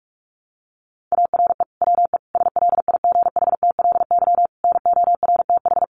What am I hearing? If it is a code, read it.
Morse code, heard as a keyed tone, about 44 wpm, 700 Hz. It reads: ARE WE SLIGHTLY NORTH
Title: Sentences from Top 1300 Words 45wpm